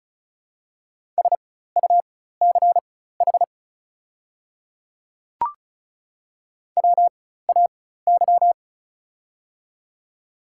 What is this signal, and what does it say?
Morse code, heard as a keyed tone, about 33 wpm, spaced as for 16 wpm, 700 Hz. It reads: SUCH WAY
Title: 4x Character Spacing - Top 300 Words - 35wpm